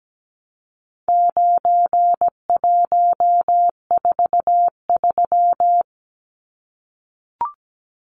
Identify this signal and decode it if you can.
Morse code, heard as a keyed tone, about 17 wpm, 700 Hz. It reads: 9143
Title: Numbers - 4-Digit 17wpm